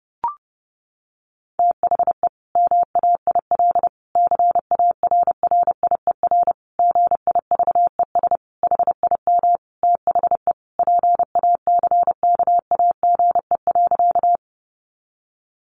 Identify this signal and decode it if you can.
Morse code, heard as a keyed tone, about 30 wpm, 700 Hz. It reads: THE MAIL CARRIER GIVES HIM THE PACKAGE.
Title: Additional Easy Phrases 30wpm